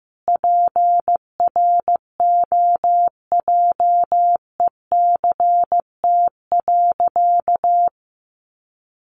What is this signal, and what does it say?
Morse code, heard as a keyed tone, about 15 wpm, 700 Hz. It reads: PROJECT.
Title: Sentences from Top 1400 Words 15wpm